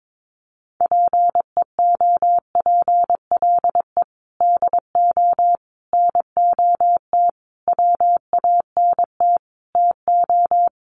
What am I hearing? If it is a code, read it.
Morse code, heard as a keyed tone, about 22 wpm, 700 Hz. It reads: PEOPLE DO NOT WANT TO
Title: Sets of 5 Words Encore2 22wpm